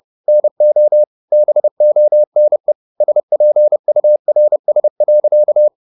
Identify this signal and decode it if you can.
Morse code, heard as a keyed tone, about 30 wpm, 600 Hz. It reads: NO BONE SPURS.